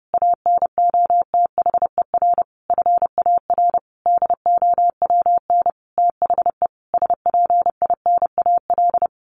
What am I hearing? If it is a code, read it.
Morse code, heard as a keyed tone, about 30 wpm, 700 Hz. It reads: ANOTHER FAR DOWN THE SPINAL
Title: A Sound Of Thunder 30wpm